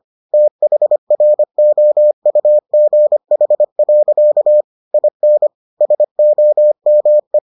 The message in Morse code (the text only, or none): THROUGH. IN SOME